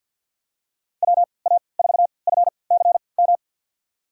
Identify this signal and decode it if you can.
Morse code, heard as a keyed tone, about 52 wpm, spaced as for 28 wpm, 700 Hz. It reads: WA4FCM
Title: USA Call Signs - 3x Character Spacing - 50wpm